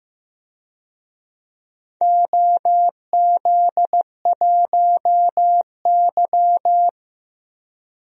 Morse code, heard as a keyed tone, about 15 wpm, 700 Hz. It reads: OZ1Y
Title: DXCC-100 Call Signs - 4 Characters - Repeated 3x - 15wpm